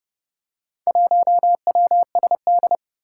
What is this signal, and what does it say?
Morse code, heard as a keyed tone, about 30 wpm, 700 Hz. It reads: 1WSD
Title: Four Challenging Letter-Number Combinations 30wpm